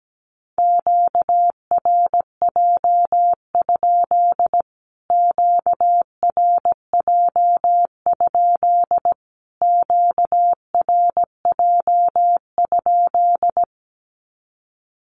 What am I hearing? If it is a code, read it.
Morse code, heard as a keyed tone, about 17 wpm, 700 Hz. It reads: QRJ? QRJ? QRJ?
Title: Q-Codes Repeated 3x 17wpm